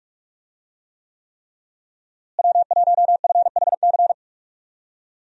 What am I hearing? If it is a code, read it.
Morse code, heard as a keyed tone, about 41 wpm, 700 Hz. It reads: W1FHC
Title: USA Call Signs - 5 Characters - Repeated 3x - 45wpm